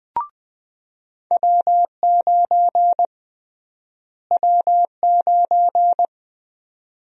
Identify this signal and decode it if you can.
Morse code, heard as a keyed tone, about 20 wpm, 700 Hz. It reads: W9 W9
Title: USA Call Sign Prefixes - Repeated 2x - 20wpm